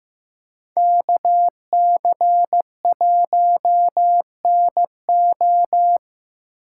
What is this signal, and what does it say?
Morse code, heard as a keyed tone, about 19 wpm, 700 Hz. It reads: KC1NO